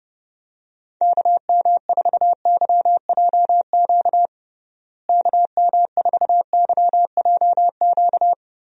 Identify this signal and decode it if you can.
Morse code, heard as a keyed tone, about 30 wpm, 700 Hz. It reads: KM4YJQ KM4YJQ
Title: USA Call Signs - 6 Characters - Repeated 3x - 30wpm